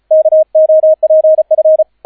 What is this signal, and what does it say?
Morse code, heard as a keyed tone, about 34 wpm, 630 Hz, with radio noise. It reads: KOPF